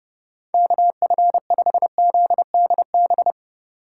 Morse code, heard as a keyed tone, about 31 wpm, 700 Hz. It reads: KF5ZDB
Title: Speed Racing - USA 2x3 Call Signs 25wpm